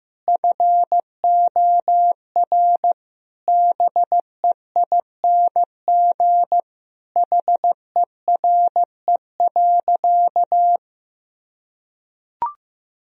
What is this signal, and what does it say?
Morse code, heard as a keyed tone, about 15 wpm, 700 Hz. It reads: FOR BEING HERE.
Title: Sentences from Top 1000 Words - Review Part 3 - 15wpm